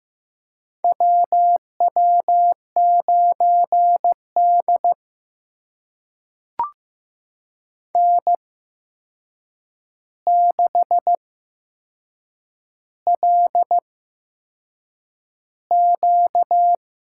Morse code, heard as a keyed tone, about 15 wpm, spaced as for 6 wpm, 700 Hz. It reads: WW9D N6LQ
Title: USA Call Signs - 4 Characters - 8x Character Spacing - 15wpm